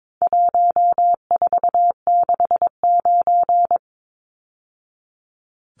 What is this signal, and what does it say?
Morse code, heard as a keyed tone, about 22 wpm, 700 Hz. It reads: 1469
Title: Numbers - 4-Digit 22wpm